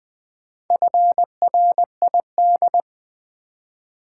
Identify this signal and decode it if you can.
Morse code, heard as a keyed tone, about 20 wpm, 700 Hz. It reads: FRID
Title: Build A Word - 2 Letters At A Time - 6-Letter Words - 20wpm